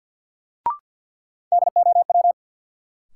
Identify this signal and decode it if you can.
Morse code, heard as a keyed tone, about 49 wpm, 700 Hz. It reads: DOW
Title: Sets of 5 Words - New Word Drill - 50wpm